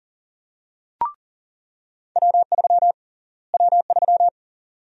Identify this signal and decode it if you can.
Morse code, heard as a keyed tone, about 40 wpm, 700 Hz. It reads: W3 W3
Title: USA Call Sign Prefixes - Repeated 2x - 40wpm